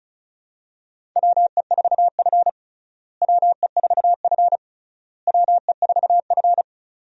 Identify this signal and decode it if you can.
Morse code, heard as a keyed tone, about 36 wpm, 700 Hz. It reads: WE4F WE4F WE4F
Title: USA Call Signs - 4 Characters - Repeated 3x - 35wpm